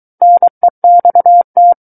Morse code, heard as a keyed tone, about 23 wpm, 700 Hz. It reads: NEXT
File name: Book One / 14-Chapter 14 - 23 WPM